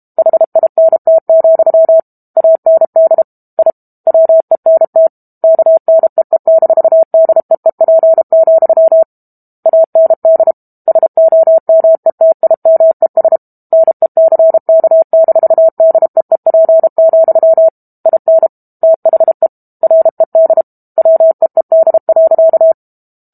Morse code, heard as a keyed tone, about 32 wpm, 650 Hz. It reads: HINT, AND I WENT KNEE-DEEP, AND SOMETIMES NECK-DEEP, IN THE RED WEED.
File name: Book Two / 06-Chapter 6 - 32 WPM